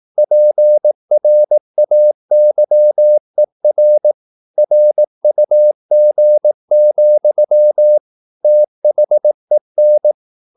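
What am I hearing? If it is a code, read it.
Morse code, heard as a keyed tone, about 18 wpm, 600 Hz. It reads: PRAYER RUG, THEN
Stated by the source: Morse practice at 18 wpm, English text, from CW med Gnister.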